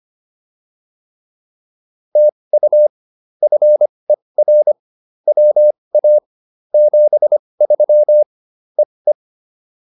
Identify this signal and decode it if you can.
Morse code, heard as a keyed tone, about 25 wpm, spaced as for 20 wpm, 600 Hz. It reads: TU FER WA 73 EE